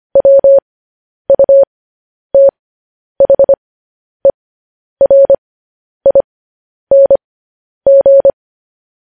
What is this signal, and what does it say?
Morse code, heard as a keyed tone, about 25 wpm, spaced as for 10 wpm, 550 Hz. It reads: WUTHERING
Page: news headlines in morse code at 10 wpm